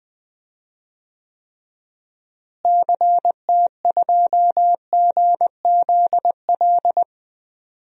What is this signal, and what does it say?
Morse code, heard as a keyed tone, about 20 wpm, 700 Hz. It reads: CT2GZL
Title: DXCC-100 Call Signs - 6 Characters - Repeated 2x - 20wpm